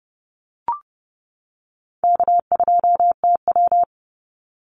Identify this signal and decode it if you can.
Morse code, heard as a keyed tone, about 30 wpm, 700 Hz. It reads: K2TW
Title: Call Signs - 30wpm